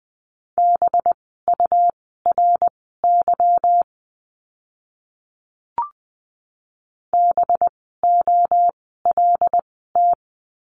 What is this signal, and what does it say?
Morse code, heard as a keyed tone, about 20 wpm, spaced as for 14 wpm, 700 Hz. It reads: BURY BOLT
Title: 185 Lesson 30f - Letter b - Words 20wpm